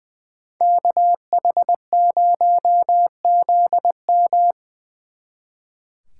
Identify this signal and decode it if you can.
Morse code, heard as a keyed tone, about 23 wpm, 700 Hz. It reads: KH0ZM
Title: Speed Racing - USA 2x2 Call Signs 20wpm